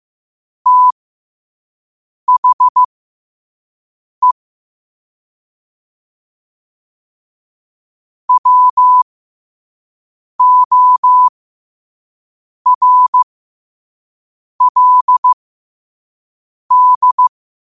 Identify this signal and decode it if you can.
Morse code, heard as a keyed tone, about 15 wpm, spaced as for 5 wpm, 1000 Hz. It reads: THE WORLD
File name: qotdfiles / 5wpm